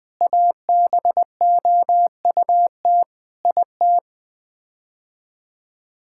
Morse code, heard as a keyed tone, about 20 wpm, 700 Hz. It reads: ABOUT IT
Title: Sets of 8 Words - 20wpm